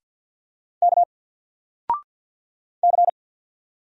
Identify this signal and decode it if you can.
Morse code, heard as a keyed tone, about 50 wpm, 700 Hz. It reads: K C
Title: Single Letter-Number 50wpm